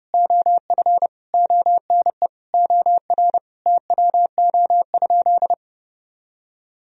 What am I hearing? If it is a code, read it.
Morse code, heard as a keyed tone, about 30 wpm, 700 Hz. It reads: OF ONE OR TWO?
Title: Sentences from Top 1000 Words - Review Part 4 - 30wpm